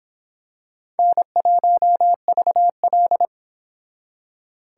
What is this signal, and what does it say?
Morse code, heard as a keyed tone, about 25 wpm, 700 Hz. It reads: N1VL